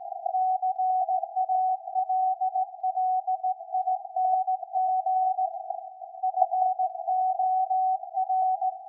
Morse code, heard as a keyed tone, about 17 wpm, 750 Hz. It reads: CALLING FOR
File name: iz7slz / VLF / SAQ 13-02-2023